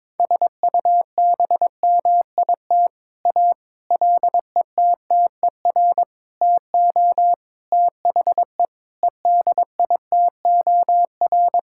Morse code, heard as a keyed tone, about 22 wpm, 700 Hz. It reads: SUBMIT A LETTER TO THE EDITOR
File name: practice / 22wpm / Sets of 6 Words - 22wpm